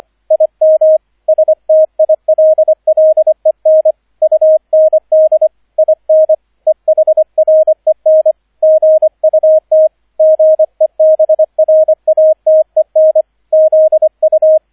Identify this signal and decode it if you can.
Morse code, heard as a keyed tone, about 25 wpm, 630 Hz, with radio noise. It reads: IM STILLEN UND IN EHREN GUT GEBRATEN ZU